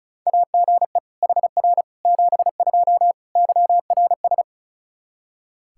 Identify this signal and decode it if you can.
Morse code, heard as a keyed tone, about 34 wpm, 700 Hz. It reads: AGE HR 72 YRS
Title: Practice QSO - Exchange 3 - More Info - 35wpm